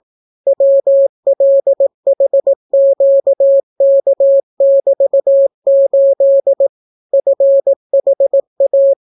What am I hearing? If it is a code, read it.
Morse code, heard as a keyed tone, about 18 wpm, 550 Hz. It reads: WLHQK<BT>8 FHA